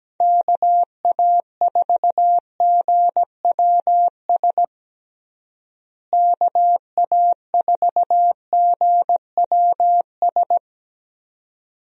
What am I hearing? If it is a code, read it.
Morse code, heard as a keyed tone, about 17 wpm, 700 Hz. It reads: KA4GWS KA4GWS
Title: USA Call Signs - 6 Characters - Repeated 2x - 17wpm